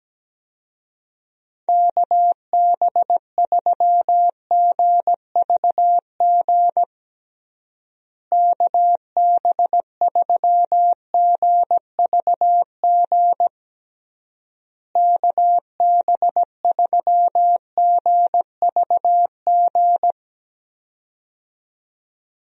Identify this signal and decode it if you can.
Morse code, heard as a keyed tone, about 17 wpm, 700 Hz. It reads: KB3GVG KB3GVG KB3GVG
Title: USA Call Signs - 6 Characters - Repeated 3x - 17wpm